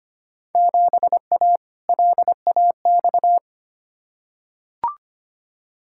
Morse code, heard as a keyed tone, about 25 wpm, 700 Hz. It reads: 7A LAX
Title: Field Day Exchanges Encore - 25wpm